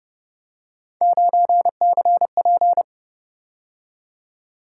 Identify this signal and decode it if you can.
Morse code, heard as a keyed tone, about 29 wpm, 700 Hz. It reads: 9CP
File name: Speed-Racing - Three Letter-Number Combinations - 25wpm